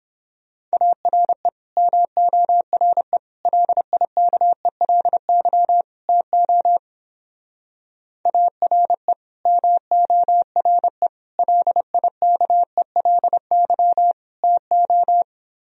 Morse code, heard as a keyed tone, about 28 wpm, 700 Hz. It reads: ARE MORE LIKELY TO ARE MORE LIKELY TO